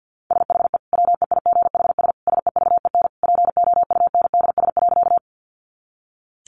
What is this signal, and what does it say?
Morse code, heard as a keyed tone, about 50 wpm, 700 Hz. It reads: SHE WEIGHS SEVEN POUNDS.
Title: Easy Phrases 50wpm